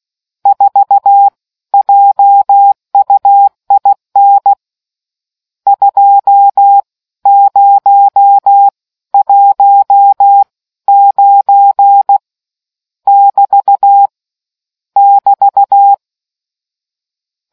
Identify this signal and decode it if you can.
Morse code, heard as a keyed tone, about 16 wpm, spaced as for 13 wpm, 800 Hz. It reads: 4 JUIN 2019 <BT> <BT>